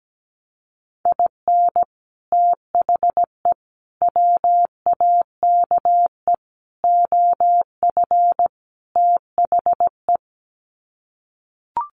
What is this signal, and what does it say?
Morse code, heard as a keyed tone, about 17 wpm, 700 Hz. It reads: IN THE WAKE OF THE
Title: Sets of 5 Words 17wpm